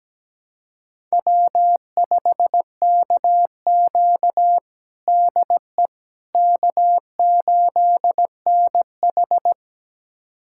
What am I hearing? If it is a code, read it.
Morse code, heard as a keyed tone, about 17 wpm, 700 Hz. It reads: W5KQ DE K8NH
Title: USA Callsign DE Callsign - 4 Characters - 17wpm